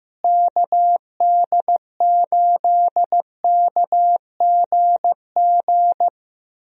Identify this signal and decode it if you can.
Morse code, heard as a keyed tone, about 15 wpm, 700 Hz. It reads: KD8KGG
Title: USA Call Signs - 6 Characters - Repeated 3x - 15wpm